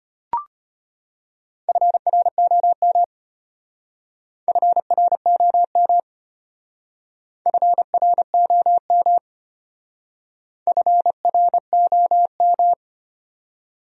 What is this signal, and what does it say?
Morse code, heard as a keyed tone, about 31 wpm, 700 Hz. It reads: FROM FROM FROM FROM